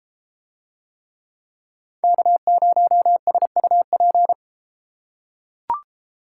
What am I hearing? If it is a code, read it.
Morse code, heard as a keyed tone, about 31 wpm, 700 Hz. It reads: K0SUP